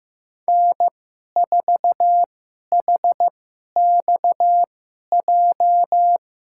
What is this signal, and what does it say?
Morse code, heard as a keyed tone, about 15 wpm, spaced as for 11 wpm, 700 Hz. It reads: N4HXJ
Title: USA Call Signs - 5 Characters - 2x Character Spacing - 15wpm